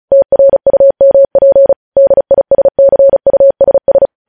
Morse code, heard as a keyed tone, about 36 wpm, 550 Hz. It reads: TRUMP DISCUSS
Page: News headlines in Morse code at 35 wpm